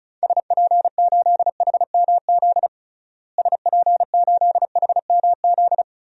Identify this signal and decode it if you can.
Morse code, heard as a keyed tone, about 34 wpm, 700 Hz. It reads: SP8HMZ SP8HMZ
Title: DXCC-100 Call Signs - 6 Characters - Repeated 3x - 35wpm